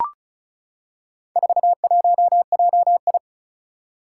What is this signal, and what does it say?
Morse code, heard as a keyed tone, about 36 wpm, 700 Hz. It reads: 41JI
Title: Four Challenging Letter-Number Combinations 35wpm